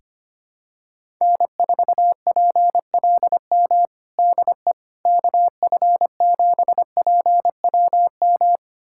Morse code, heard as a keyed tone, about 25 wpm, 700 Hz. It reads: N4PLM DE KF7PWM
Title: USA Callsign DE Callsign 25wpm